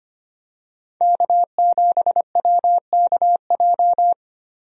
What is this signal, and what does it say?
Morse code, heard as a keyed tone, about 25 wpm, 700 Hz. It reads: K7WKJ
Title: USA Call Signs - 5 Characters - 25wpm